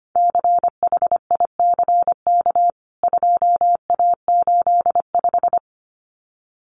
Morse code, heard as a keyed tone, about 25 wpm, 700 Hz. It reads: CHICK 2A85
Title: CWT Exchange 25wpm